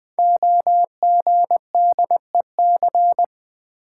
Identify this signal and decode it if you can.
Morse code, heard as a keyed tone, about 20 wpm, 700 Hz. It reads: OGDEC
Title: Code Groups - 5 Letters - 20wpm